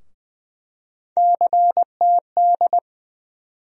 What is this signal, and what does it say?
Morse code, heard as a keyed tone, about 20 wpm, 700 Hz. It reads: CTD
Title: USA Call Sign Postfixes - 3 Letters 20wpm